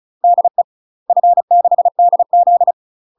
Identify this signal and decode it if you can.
Morse code, heard as a keyed tone, about 35 wpm, 700 Hz. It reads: DE F6DZ
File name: UK MASTER QSOs TRAINING / 35WPM / UK MASTER QSOs 0038